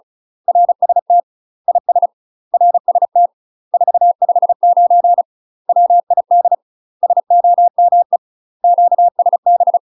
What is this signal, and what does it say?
Morse code, heard as a keyed tone, about 34 wpm, 700 Hz. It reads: RST IS RST 459 WID SOME QSB
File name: UK MASTER QSOs TRAINING / 35WPM / UK MASTER QSOs 0037